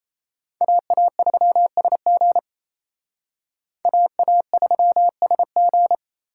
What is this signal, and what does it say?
Morse code, heard as a keyed tone, about 34 wpm, 700 Hz. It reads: AA3SG AA3SG
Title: Speed Racing - USA 2x2 Call Signs 28wpm